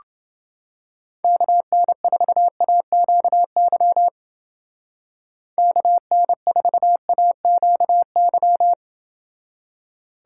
Morse code, heard as a keyed tone, about 28 wpm, 700 Hz. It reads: KN4AQY KN4AQY